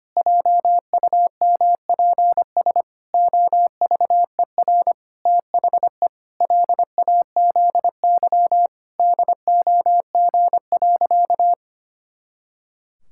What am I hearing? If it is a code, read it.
Morse code, heard as a keyed tone, about 25 wpm, 700 Hz. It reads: JUMPS OVER THE LAZY DOG.
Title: Pangram Challenge v2 - 25wpm